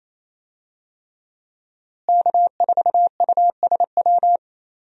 Morse code, heard as a keyed tone, about 27 wpm, 700 Hz. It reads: K4USW
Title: USA Call Signs - 5 Characters - Repeated 3x - 28wpm